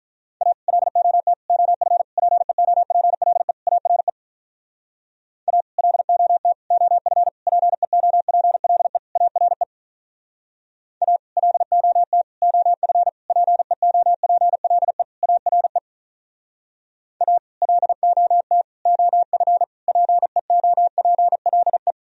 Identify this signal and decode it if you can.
Morse code, heard as a keyed tone, about 45 wpm, 700 Hz. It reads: A LOT OF PEOPLE ARE A LOT OF PEOPLE ARE A LOT OF PEOPLE ARE A LOT OF PEOPLE